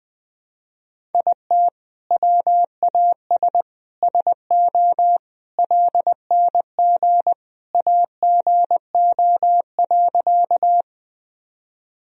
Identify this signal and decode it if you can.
Morse code, heard as a keyed tone, about 21 wpm, 700 Hz. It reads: IT WAS SO LNG AGO.
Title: Sentences from Top 500 Words and Abbreviations 20wpm